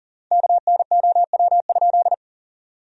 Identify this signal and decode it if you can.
Morse code, heard as a keyed tone, about 40 wpm, 700 Hz. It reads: KNOW?
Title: 3x Word-Spacing - Sentences from Top 100 Words - 40wpm